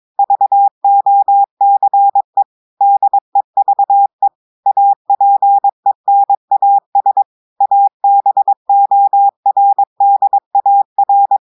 Morse code, heard as a keyed tone, about 22 wpm, 800 Hz. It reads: VOCE DEVE APENAS ABORDAR
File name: audios / portugais / portugais texte en clair 2 vitesse 22 mots minutes